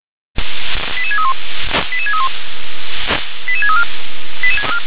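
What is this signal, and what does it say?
Five-tone selective call (ZVEI), heard as quick runs of five tones, tones of about 70 ms. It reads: 88631 88631 88634
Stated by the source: ZVEI five-tone call 88912 received by SDR.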